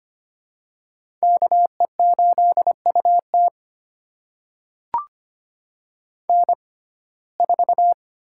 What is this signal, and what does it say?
Morse code, heard as a keyed tone, about 25 wpm, spaced as for 18 wpm, 700 Hz. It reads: KE8UT N4
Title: USA Call Signs - 5 Characters - 6x Character Spacing - 25wpm